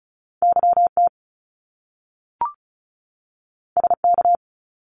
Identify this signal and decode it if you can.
Morse code, heard as a keyed tone, about 36 wpm, 700 Hz. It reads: YT SK
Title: Canadian Provinces and Territories 35wpm